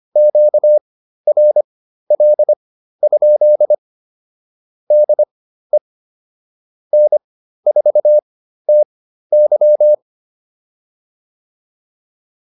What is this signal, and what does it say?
Morse code, heard as a keyed tone, about 25 wpm, spaced as for 13 wpm, 600 Hz. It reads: QRL? DE N4TY